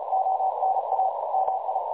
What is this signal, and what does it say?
Morse code, heard as a keyed tone, about 33 wpm, 760 Hz, with noise. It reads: ONA MI
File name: dlazaawansowanych / nagraniazpasma / websdr recording 3565.0kHz